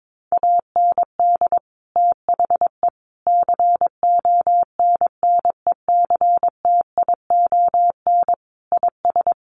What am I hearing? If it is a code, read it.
Morse code, heard as a keyed tone, about 22 wpm, 700 Hz. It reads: AND THE CONNECTION IS
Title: Sentences from Top 1300 Words 22wpm